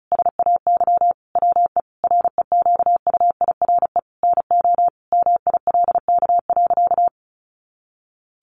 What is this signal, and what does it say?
Morse code, heard as a keyed tone, about 34 wpm, 700 Hz. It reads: SAY WE REQUIRE NO MILK.
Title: Sentences from Top 1500 Words 35wpm